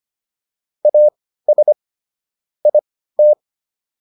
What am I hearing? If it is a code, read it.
Morse code, heard as a keyed tone, about 25 wpm, spaced as for 15 wpm, 600 Hz. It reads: AS IT